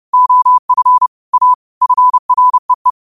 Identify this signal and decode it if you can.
Morse code, heard as a keyed tone, about 30 wpm, 1000 Hz. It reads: OF A FREE
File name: qotdfiles / 30wpm